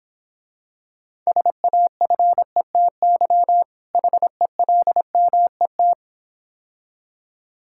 Morse code, heard as a keyed tone, about 26 wpm, 700 Hz. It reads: SAFETY HELMET